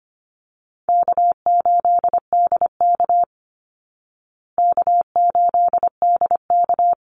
Morse code, heard as a keyed tone, about 25 wpm, 700 Hz. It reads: K8DK K8DK
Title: Easy Call Signs Repeated 3x - 25wpm